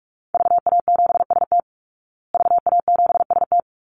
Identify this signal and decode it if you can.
Morse code, heard as a keyed tone, about 44 wpm, 700 Hz. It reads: VA7ST VA7ST
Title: Call Signs Repeated 3x - 45wpm